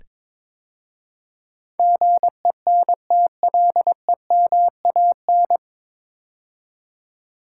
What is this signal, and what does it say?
Morse code, heard as a keyed tone, about 22 wpm, 700 Hz. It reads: GENTLEMAN
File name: 9 Letter Words - First 3 Letters Given - 22wpm